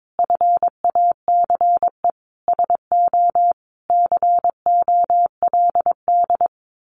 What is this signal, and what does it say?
Morse code, heard as a keyed tone, about 22 wpm, 700 Hz. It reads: FACE SO COLD
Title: Sentences from Top 1400 Words 22wpm